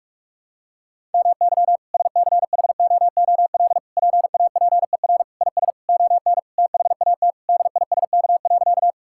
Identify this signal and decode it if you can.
Morse code, heard as a keyed tone, about 46 wpm, 700 Hz. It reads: MY SCHOOL PAPER IS ON THAT DISK.
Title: Sentences from Top 1000 Words - Review Part 6 - 45wpm